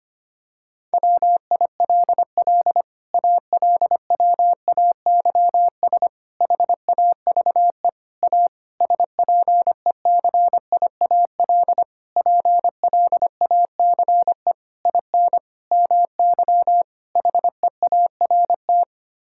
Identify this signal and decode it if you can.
Morse code, heard as a keyed tone, about 25 wpm, 700 Hz. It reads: WILL ALWAYS HAVE A SPECIAL PLACE IN MY HEART
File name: Sets of 9 Words - 25wpm